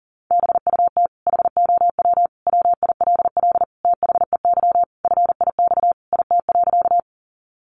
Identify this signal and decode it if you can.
Morse code, heard as a keyed tone, about 40 wpm, 700 Hz. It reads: BUT HOW WILL THEY FIX IT.
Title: Sentences from Top 1400 Words 40wpm